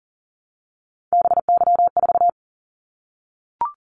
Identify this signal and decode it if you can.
Morse code, heard as a keyed tone, about 42 wpm, 700 Hz. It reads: BY4
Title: Callsign Prefixes for Top 100 DXCC - 40wpm